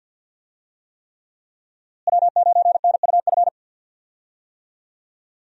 Morse code, heard as a keyed tone, about 50 wpm, 700 Hz. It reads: W9NUF
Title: USA Call Signs - 5 Characters - Repeated 3x - 50wpm